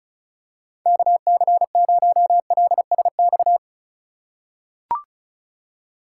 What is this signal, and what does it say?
Morse code, heard as a keyed tone, about 36 wpm, 700 Hz. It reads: KC0LSX
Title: USA Call Signs - 6 Characters - 2x Character Spacing - 35wpm